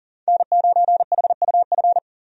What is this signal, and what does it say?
Morse code, heard as a keyed tone, about 40 wpm, 700 Hz. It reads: N9HUF
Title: USA Call Signs - 5 Characters - 40wpm